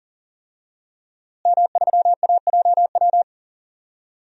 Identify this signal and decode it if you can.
Morse code, heard as a keyed tone, about 40 wpm, 700 Hz. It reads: M3AJW
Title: DXCC-100 Call Signs - 5 Characters - Repeated 2x - 40wpm